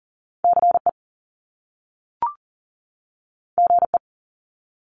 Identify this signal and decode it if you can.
Morse code, heard as a keyed tone, about 41 wpm, 700 Hz. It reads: CE GE
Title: Top Two Letter Combinations in Words 40wpm